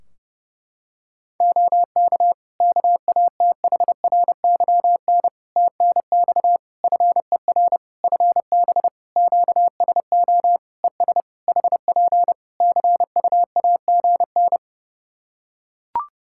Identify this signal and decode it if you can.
Morse code, heard as a keyed tone, about 30 wpm, 700 Hz. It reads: OK KATHRYN TNX FER FB QSO ES HP CUAGN
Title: Practice QSO - Exchange 4 - Ending - 30wpm